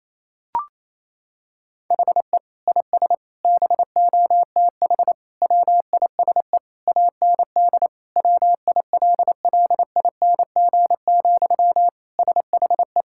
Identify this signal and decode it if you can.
Morse code, heard as a keyed tone, about 28 wpm, 700 Hz. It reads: HE IS BOTH WISE AND WILLING, SHE
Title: Sentences from Top 1600 Words 28wpm